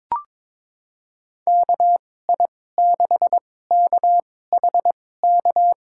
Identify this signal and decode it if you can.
Morse code, heard as a keyed tone, about 22 wpm, spaced as for 16 wpm, 700 Hz. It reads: KI6KHK
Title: USA Call Signs - 6 Characters - 2x Character Spacing - 22wpm